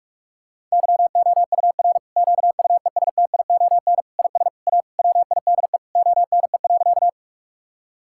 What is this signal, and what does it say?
Morse code, heard as a keyed tone, about 45 wpm, 700 Hz. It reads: YOUR QUESTION IS A WIDE ONE.
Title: Sentences from Top 1000 Words - Review Part 6 - 45wpm